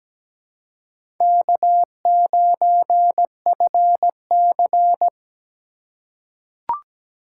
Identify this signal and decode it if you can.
Morse code, heard as a keyed tone, about 17 wpm, 700 Hz. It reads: K9FC
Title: USA Call Signs - 4 Characters - 6x Character Spacing - 17wpm